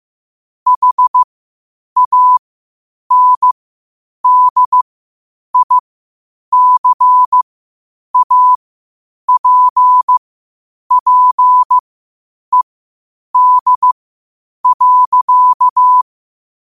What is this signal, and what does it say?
Morse code, heard as a keyed tone, about 15 wpm, spaced as for 8 wpm, 1000 Hz. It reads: HANDICAPPED.
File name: qotdfiles / 8wpm